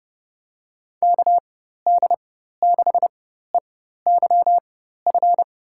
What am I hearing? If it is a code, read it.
Morse code, heard as a keyed tone, about 30 wpm, spaced as for 14 wpm, 700 Hz. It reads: KD6EYF
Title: USA Call Signs - 6 Characters - 4x Character Spacing - 30wpm